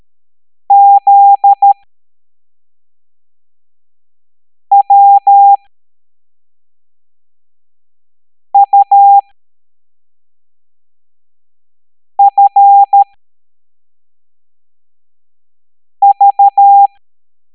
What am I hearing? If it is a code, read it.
Morse code, heard as a keyed tone, about 13 wpm, spaced as for 3 wpm, 800 Hz. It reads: ZWUFV